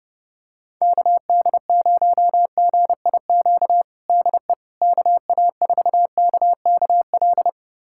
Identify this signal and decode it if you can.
Morse code, heard as a keyed tone, about 30 wpm, 700 Hz. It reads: KD0GIQ DE KA4KKL
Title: USA Callsign DE Callsign - 6 Characters - 30wpm